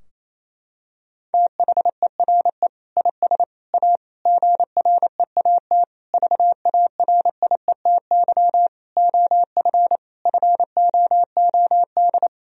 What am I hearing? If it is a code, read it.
Morse code, heard as a keyed tone, about 28 wpm, 700 Hz. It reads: THERE IS A GREAT VARIETY OF FOOD
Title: Sentences from Top 1000 Words - Review Part 4 - 28wpm